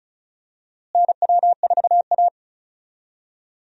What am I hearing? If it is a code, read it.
Morse code, heard as a keyed tone, about 36 wpm, 700 Hz. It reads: NW4A
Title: USA Call Signs - 4 Characters - Repeated 3x - 35wpm